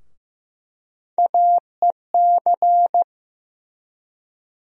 Morse code, heard as a keyed tone, about 15 wpm, 700 Hz. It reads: AEC
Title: USA Call Sign Postfixes - 3 Letters 15wpm